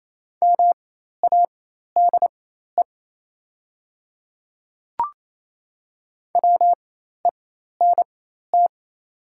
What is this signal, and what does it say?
Morse code, heard as a keyed tone, about 28 wpm, spaced as for 13 wpm, 700 Hz. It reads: MADE WENT
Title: 4x Character Spacing - Top 300 Words - 28wpm